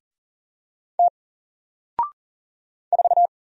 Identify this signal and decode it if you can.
Morse code, heard as a keyed tone, about 40 wpm, 700 Hz. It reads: T 4
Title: Single Letter-Number Alt 40wpm